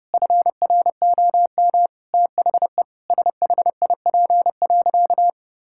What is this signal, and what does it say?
Morse code, heard as a keyed tone, about 30 wpm, 700 Hz. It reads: FROM THE SHIP.
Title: Sentences from Top 1000 Words - Review Part 5 - 30wpm